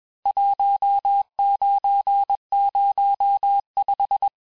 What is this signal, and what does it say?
Morse code, heard as a keyed tone, about 21 wpm, 800 Hz. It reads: 1905
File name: teste-clair-vitesse-15mpm-puis-20mpm